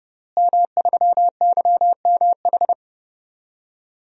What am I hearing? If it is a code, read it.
Morse code, heard as a keyed tone, about 30 wpm, 700 Hz. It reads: M3YMH
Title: DXCC-100 Call Signs - 5 Characters - Repeated 2x - 30wpm